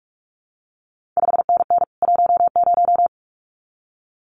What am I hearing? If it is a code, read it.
Morse code, heard as a keyed tone, about 46 wpm, 700 Hz. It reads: 5NN 10
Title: CQ WW DX Contest Exchange Encore - 45wpm